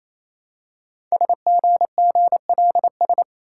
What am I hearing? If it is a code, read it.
Morse code, heard as a keyed tone, about 27 wpm, 700 Hz. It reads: SGGLS
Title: Code Groups - 5 Letters - 28wpm